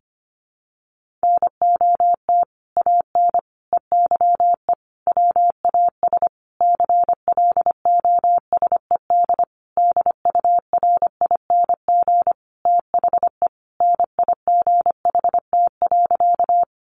Morse code, heard as a keyed tone, about 25 wpm, 700 Hz. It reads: NOT AN EYE WAS CLOSED DURING THE NIGHT.
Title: Sentences from Top 1400 Words 25wpm